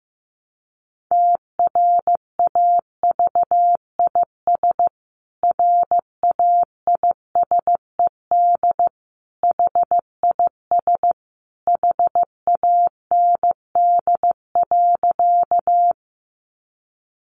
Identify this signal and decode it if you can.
Morse code, heard as a keyed tone, about 15 wpm, 700 Hz. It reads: TRAVIS RAISED HIS HAND.
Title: A Sound Of Thunder 15wpm